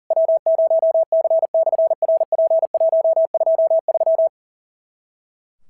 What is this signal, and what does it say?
Morse code, heard as a keyed tone, about 41 wpm, 650 Hz. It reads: W0C/RP123
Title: SOTA QSO Practice 40wpm